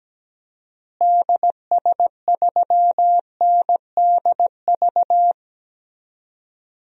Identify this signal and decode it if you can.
Morse code, heard as a keyed tone, about 17 wpm, 700 Hz. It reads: DS3NDV
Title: DXCC-100 Call Signs - 6 Characters - Repeated 2x - 17wpm